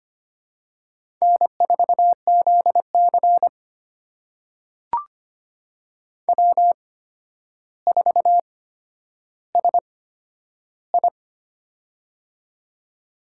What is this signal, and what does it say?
Morse code, heard as a keyed tone, about 25 wpm, spaced as for 11 wpm, 700 Hz. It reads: N4ZC W4SI
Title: USA Call Signs - 4 Characters - 8x Character Spacing - 25wpm